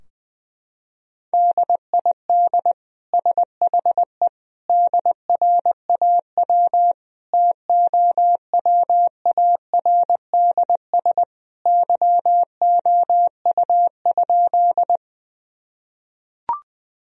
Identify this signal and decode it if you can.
Morse code, heard as a keyed tone, about 21 wpm, 700 Hz. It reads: DID SHE DRAW TOWARDS YOU?
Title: Sentences from Top 1000 Words - Review Part 2 - 20wpm